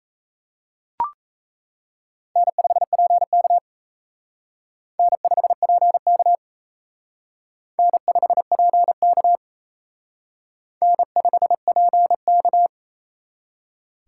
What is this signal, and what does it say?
Morse code, heard as a keyed tone, about 34 wpm, 700 Hz. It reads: N5PK N5PK N5PK N5PK
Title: Speed Racing - USA 1x2 Call Signs 28wpm